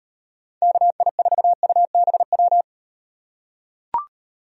Morse code, heard as a keyed tone, about 37 wpm, 700 Hz. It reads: KI4UBW